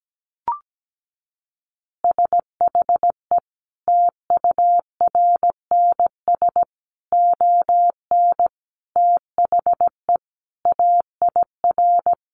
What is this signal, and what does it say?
Morse code, heard as a keyed tone, about 17 wpm, 700 Hz. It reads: SHE TURNS ON THE AIR
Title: Easy Phrases 17wpm